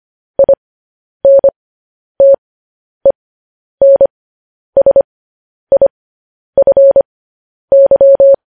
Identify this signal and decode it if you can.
Morse code, heard as a keyed tone, about 25 wpm, spaced as for 10 wpm, 550 Hz. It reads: INTENSIFY